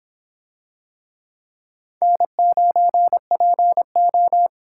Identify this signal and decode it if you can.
Morse code, heard as a keyed tone, about 23 wpm, 700 Hz. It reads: N9PO